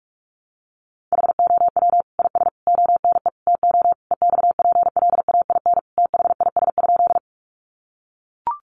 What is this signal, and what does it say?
Morse code, heard as a keyed tone, about 44 wpm, 700 Hz. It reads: HOW IS ONE TO EXPLAIN THIS?
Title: Sentences from Top 1100 Words 45wpm